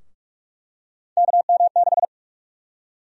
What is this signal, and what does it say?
Morse code, heard as a keyed tone, about 46 wpm, 700 Hz. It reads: KM6
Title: USA Call Sign Prefixes - 45wpm